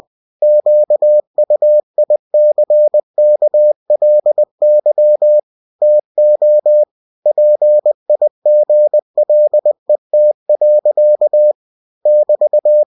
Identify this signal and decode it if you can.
Morse code, heard as a keyed tone, about 20 wpm, 600 Hz. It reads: QUICKLY TO PIGLET. <BT>
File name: cw-books / winnie-the-pooh / winnie-20wpm-page-0081